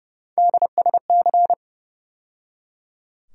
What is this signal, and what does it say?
Morse code, heard as a keyed tone, about 32 wpm, 700 Hz. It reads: DSC
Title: USA Call Sign Postfixes - 3 Letters 30wpm